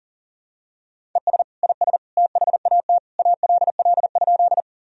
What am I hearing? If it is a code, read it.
Morse code, heard as a keyed tone, about 40 wpm, 700 Hz. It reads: ES IS THAT ALL?
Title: Sentences from Top 100 Words and Abbreviations 40wpm